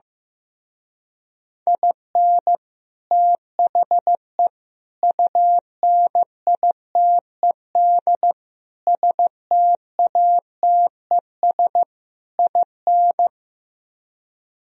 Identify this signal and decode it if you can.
Morse code, heard as a keyed tone, about 15 wpm, 690 Hz. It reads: IN THE UNITED STATES IN
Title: Speed-Racing - Sets of 5 Words 15wpm